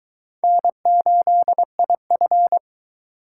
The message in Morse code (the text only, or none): N8IF